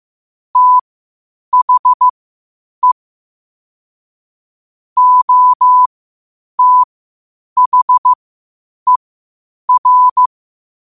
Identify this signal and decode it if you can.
Morse code, heard as a keyed tone, about 15 wpm, spaced as for 8 wpm, 1000 Hz. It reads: THE OTHER
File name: qotdfiles / 8wpm